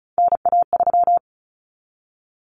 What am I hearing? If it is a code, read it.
Morse code, heard as a keyed tone, about 34 wpm, 700 Hz. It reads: NA3
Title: Common Cut Numbers - 3-Digit 35wpm